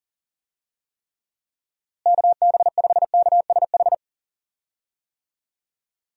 Morse code, heard as a keyed tone, about 38 wpm, 700 Hz. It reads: KB5KSH
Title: USA Call Signs - 6 Characters - Repeated 2x - 40wpm